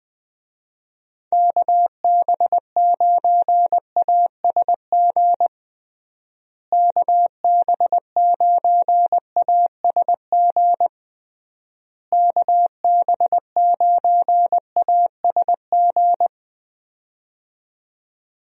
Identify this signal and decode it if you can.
Morse code, heard as a keyed tone, about 20 wpm, 700 Hz. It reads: KB9ASG KB9ASG KB9ASG
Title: USA Call Signs - 6 Characters - Repeated 3x - 20wpm